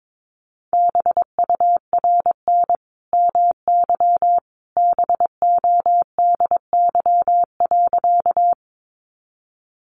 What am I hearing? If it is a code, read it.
Morse code, heard as a keyed tone, about 22 wpm, 700 Hz. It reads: BURN MY BODY.
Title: Sentences from Top 1300 Words 22wpm